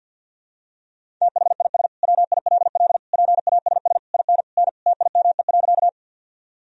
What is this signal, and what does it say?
Morse code, heard as a keyed tone, about 51 wpm, 700 Hz. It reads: THIS WILL PASS IN N TIME.
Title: Sentences from Top 400 Words and Abbreviations 50wpm